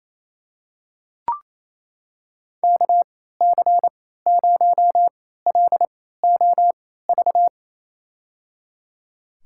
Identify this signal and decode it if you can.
Morse code, heard as a keyed tone, about 28 wpm, spaced as for 16 wpm, 700 Hz. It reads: KC0LOV
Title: USA Call Signs - 3x Character Spacing - 28wpm